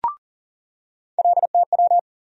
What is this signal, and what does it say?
Morse code, heard as a keyed tone, about 40 wpm, 700 Hz. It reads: LTW
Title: USA Call Sign Postfixes - 3 Letters 40wpm